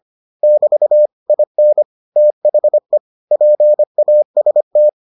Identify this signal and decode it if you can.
Morse code, heard as a keyed tone, about 25 wpm, 600 Hz. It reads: <BT> IN THE PAST